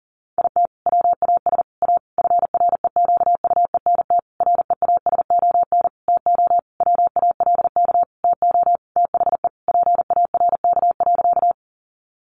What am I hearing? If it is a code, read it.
Morse code, heard as a keyed tone, about 40 wpm, 700 Hz. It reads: IT WAS A FREQUENT REASON TO WALK TO THE PARK.
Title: Sentences from Top 1400 Words 40wpm